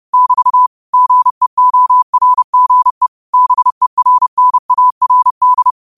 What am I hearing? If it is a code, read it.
Morse code, heard as a keyed tone, about 30 wpm, 1000 Hz. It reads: <BT> GEORGE BERNARD